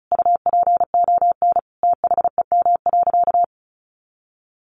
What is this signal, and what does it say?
Morse code, heard as a keyed tone, about 34 wpm, 700 Hz. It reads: UPON THEM.
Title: Sentences from Top 1200 Words 35wpm